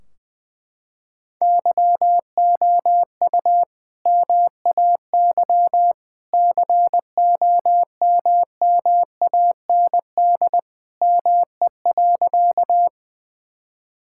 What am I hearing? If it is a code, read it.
Morse code, heard as a keyed tone, about 21 wpm, 700 Hz. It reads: YOU MAY COMMAND ME.
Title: Sentences from Top 1600 Words 20wpm